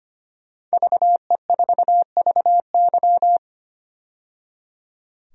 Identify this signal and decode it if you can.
Morse code, heard as a keyed tone, about 25 wpm, 700 Hz. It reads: VE4VY
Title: DXCC-100 Call Signs - 5 Characters - 25wpm